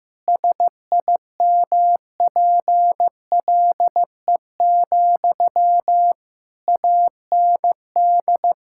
Morse code, heard as a keyed tone, about 16 wpm, 700 Hz. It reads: SIMPLE, AND
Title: Sentences from Top 1000 Words - Review Part 6 - 15wpm